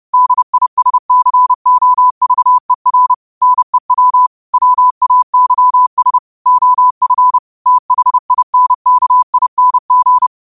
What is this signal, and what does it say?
Morse code, heard as a keyed tone, about 30 wpm, 1000 Hz. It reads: DISCOVER NEW WAYS OF THINKING
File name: qotdfiles / 30wpm